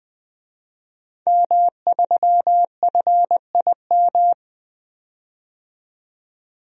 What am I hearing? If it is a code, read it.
Morse code, heard as a keyed tone, about 20 wpm, 700 Hz. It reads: M3FIM
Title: DXCC-100 Call Signs - 5 Characters - 20wpm